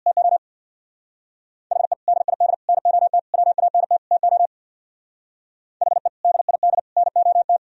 Morse code, heard as a keyed tone, about 64 wpm, 700 Hz. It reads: TO HE DID NOT WANT TO HE DID NOT